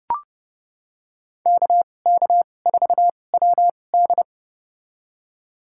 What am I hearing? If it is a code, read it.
Morse code, heard as a keyed tone, about 30 wpm, spaced as for 22 wpm, 700 Hz. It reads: KK4WD
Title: USA Call Signs - 5 Characters - 2x Character Spacing - 30wpm